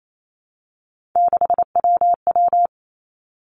Morse code, heard as a keyed tone, about 28 wpm, 690 Hz. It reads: 6WW
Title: Three Challenging Letter-Number Combinations 28wpm